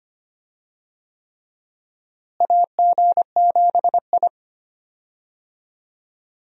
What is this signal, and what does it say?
Morse code, heard as a keyed tone, about 25 wpm, 700 Hz. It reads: AG7I
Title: USA Call Signs - 4 Characters - Repeated 3x - 25wpm